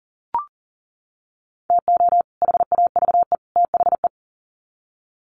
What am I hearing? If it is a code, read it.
Morse code, heard as a keyed tone, about 40 wpm, 700 Hz. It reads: TO HAVE THE
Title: Sets of 3 Words 40wpm